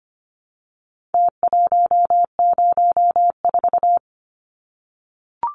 Morse code, heard as a keyed tone, about 25 wpm, 700 Hz. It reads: T104
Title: Numbers - Contest Style - 4-Digit 25wpm